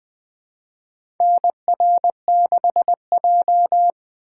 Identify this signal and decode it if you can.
Morse code, heard as a keyed tone, about 21 wpm, 700 Hz. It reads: NR6J
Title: USA Call Signs - 4 Characters - 2x Character Spacing - 20wpm